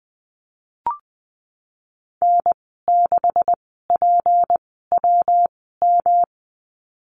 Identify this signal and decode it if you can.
Morse code, heard as a keyed tone, about 20 wpm, spaced as for 15 wpm, 700 Hz. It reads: N6PWM
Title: 167 Lesson 27g - Number 7 - Call Signs 20wpm